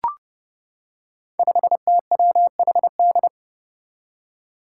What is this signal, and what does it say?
Morse code, heard as a keyed tone, about 32 wpm, 700 Hz. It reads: 5TWHD
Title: Five Challenging Letter-Number Combinations 30wpm